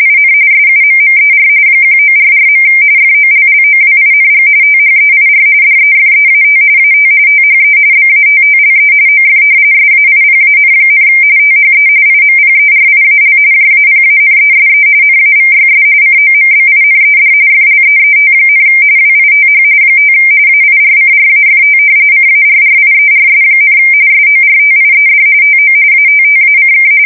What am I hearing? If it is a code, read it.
RTTY, about 45.45 baud. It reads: IN SUMMER 1944. FOLLOWING THEIR DEFEAT AT THE BATTLE OF SAIPAN, THE IMPERIAL JAPANESE NAVYS COMBINED FLEET, WITH 5 AIRCRAFT CARRIERS, SORTIED TO ATTACK
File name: fsk170 / usnavy4